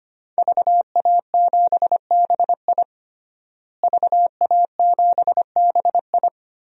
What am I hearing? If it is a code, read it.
Morse code, heard as a keyed tone, about 25 wpm, 700 Hz. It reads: VA7BI VA7BI
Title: DXCC-100 Call Signs - 5 Characters - Repeated 2x - 25wpm